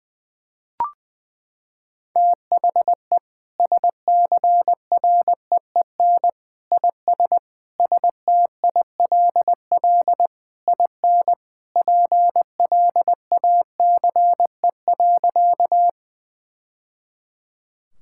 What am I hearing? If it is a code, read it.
Morse code, heard as a keyed tone, about 20 wpm, 700 Hz. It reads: THE SCREEN IS STILL IN PLACE.
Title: Sentences from Top 1000 Words - Review Part 6 - 20wpm